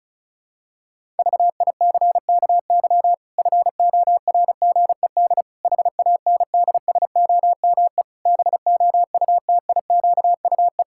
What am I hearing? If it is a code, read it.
Morse code, heard as a keyed tone, about 34 wpm, 700 Hz. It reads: VICKY FORGED HANDSOME BOUTIQUE
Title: Pangram Challenge v2 - 35wpm